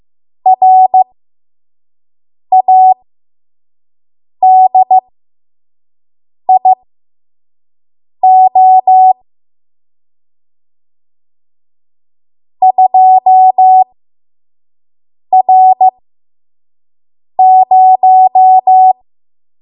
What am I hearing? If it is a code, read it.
Morse code, heard as a keyed tone, about 15 wpm, spaced as for 5 wpm, 750 Hz. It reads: RADIO 2R0
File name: Morse / 05WPM